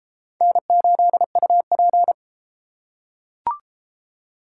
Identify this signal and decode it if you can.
Morse code, heard as a keyed tone, about 33 wpm, 700 Hz. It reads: N8UP